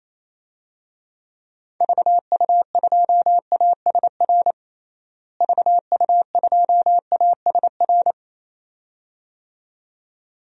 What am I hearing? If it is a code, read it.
Morse code, heard as a keyed tone, about 28 wpm, 700 Hz. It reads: VU2ASR VU2ASR
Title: DXCC-100 Call Signs - 6 Characters - Repeated 2x - 28wpm